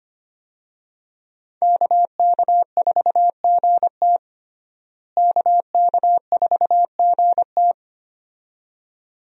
Morse code, heard as a keyed tone, about 25 wpm, 700 Hz. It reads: KK4GT KK4GT
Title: USA Call Signs - 5 Characters - Repeated 2x - 25wpm